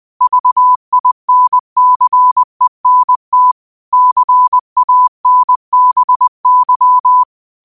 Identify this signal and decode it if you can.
Morse code, heard as a keyed tone, about 20 wpm, 1000 Hz. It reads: VINCENT CANBY